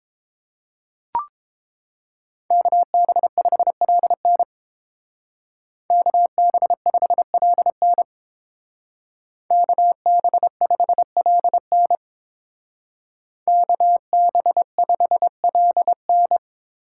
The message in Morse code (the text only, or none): KB5LN KB5LN KB5LN KB5LN